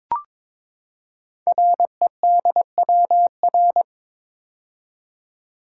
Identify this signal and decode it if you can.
Morse code, heard as a keyed tone, about 23 wpm, 700 Hz. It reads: REDWR
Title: Code Groups - 5 Letters - 22wpm